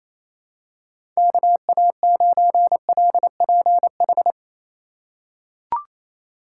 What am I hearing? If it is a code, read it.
Morse code, heard as a keyed tone, about 28 wpm, 700 Hz. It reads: KA9LPH
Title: USA Call Signs - 4x Character Spacing - 28wpm